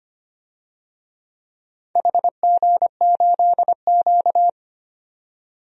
Morse code, heard as a keyed tone, about 25 wpm, 700 Hz. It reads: HG8Q
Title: DXCC-100 Call Signs - 4 Characters - 25wpm